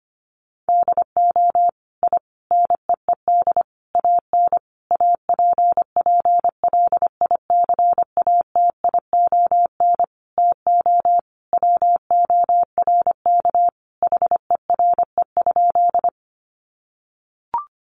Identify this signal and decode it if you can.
Morse code, heard as a keyed tone, about 25 wpm, 700 Hz. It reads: DO I NEED AN APPLICATION TO WORK HERE?
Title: Sentences from Top 1200 Words 25wpm